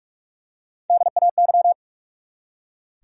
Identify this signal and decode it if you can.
Morse code, heard as a keyed tone, about 46 wpm, 700 Hz. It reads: DAY
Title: Top 100 Words 45wpm